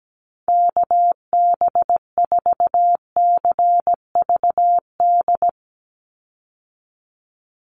Morse code, heard as a keyed tone, about 17 wpm, 700 Hz. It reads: KB4CVD
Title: USA Call Signs 17wpm